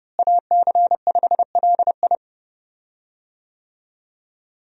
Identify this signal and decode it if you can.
Morse code, heard as a keyed tone, about 30 wpm, 700 Hz. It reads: AC5LI
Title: USA Call Signs - 5 Characters - Repeated 2x - 30wpm